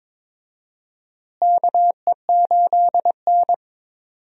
Morse code, heard as a keyed tone, about 22 wpm, 700 Hz. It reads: KE8N
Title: USA Call Signs - 4 Characters - Repeated 3x - 22wpm